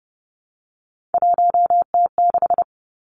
Morse code, heard as a keyed tone, about 30 wpm, 700 Hz. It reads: 1T6
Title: Three Challenging Letter-Number Combinations 30wpm